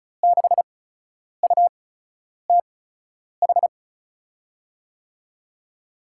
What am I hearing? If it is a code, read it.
Morse code, heard as a keyed tone, about 34 wpm, spaced as for 10 wpm, 700 Hz. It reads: 6UTH
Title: USA Call Signs - 6 Characters - 8x Character Spacing - 35wpm